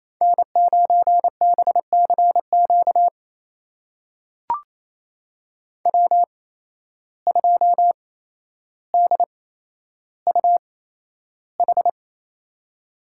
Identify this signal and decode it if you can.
Morse code, heard as a keyed tone, about 28 wpm, spaced as for 12 wpm, 700 Hz. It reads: N9BCQ W2DUH
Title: USA Call Signs - 5 Characters - 8x Character Spacing - 28wpm